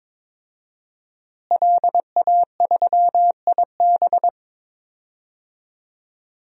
Morse code, heard as a keyed tone, about 22 wpm, 700 Hz. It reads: LA3IB
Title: DXCC-100 Call Signs - 5 Characters - Repeated 3x - 22wpm